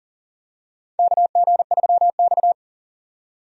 Morse code, heard as a keyed tone, about 40 wpm, 700 Hz. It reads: KG3X
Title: USA Call Signs - 4 Characters - 40wpm